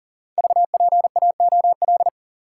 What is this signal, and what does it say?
Morse code, heard as a keyed tone, about 38 wpm, 700 Hz. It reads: VPAOL
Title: Code Groups - 5 Letters - 40wpm